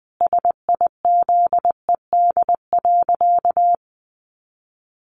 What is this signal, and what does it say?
Morse code, heard as a keyed tone, about 20 wpm, 700 Hz. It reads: SIZED.
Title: Easy Phrases 20wpm